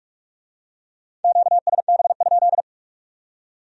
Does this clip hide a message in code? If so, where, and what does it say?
Yes: Morse code from 1.2 to 2.6 s, heard as a keyed tone, about 41 wpm, 700 Hz. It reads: QSB?